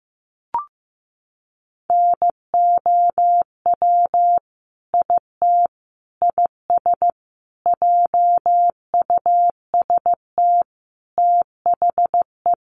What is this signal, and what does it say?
Morse code, heard as a keyed tone, about 15 wpm, 700 Hz. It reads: NOW IT IS JUST THE
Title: Sentences from Top 1300 Words 15wpm